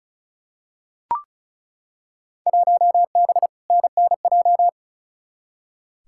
Morse code, heard as a keyed tone, about 36 wpm, 700 Hz. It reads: 1B NNJ
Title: Field Day Exchanges Encore - 35wpm